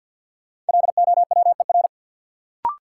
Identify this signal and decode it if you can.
Morse code, heard as a keyed tone, about 53 wpm, 700 Hz. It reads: LOWER
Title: Sentences from Top 700 Words - New Word Drill - 50wpm